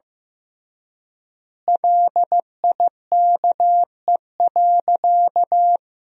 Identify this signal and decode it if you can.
Morse code, heard as a keyed tone, about 15 wpm, 700 Hz. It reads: LIKE.
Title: 3x Word-Spacing - Sentences from Top 500 Words - 15wpm